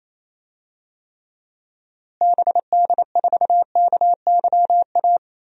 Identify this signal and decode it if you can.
Morse code, heard as a keyed tone, about 28 wpm, 700 Hz. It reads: BD4KYA
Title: DXCC-100 Call Signs - 6 Characters - Repeated 2x - 28wpm